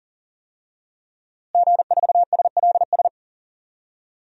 Morse code, heard as a keyed tone, about 37 wpm, 700 Hz. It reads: G4SLS
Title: DXCC-100 Call Signs - 5 Characters - Repeated 3x - 40wpm